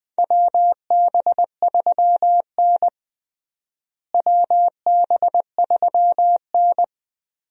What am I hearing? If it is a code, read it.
Morse code, heard as a keyed tone, about 21 wpm, 700 Hz. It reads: WB3N WB3N
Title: USA Call Signs - 4 Characters - Repeated 2x - 20wpm